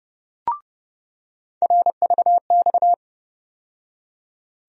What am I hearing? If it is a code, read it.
Morse code, heard as a keyed tone, about 30 wpm, 700 Hz. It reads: RVX
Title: USA Call Sign Postfixes - 3 Letters 30wpm